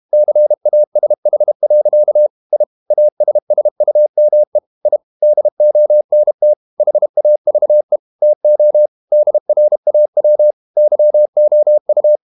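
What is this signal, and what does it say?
Morse code, heard as a keyed tone, about 32 wpm, 600 Hz. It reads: CASH. I ASSUME I DONT HAVE TO DRAW YOU
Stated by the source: Morse code practice at 32 wpm from CW med Gnister, reading Ken Follett's novel Never in English.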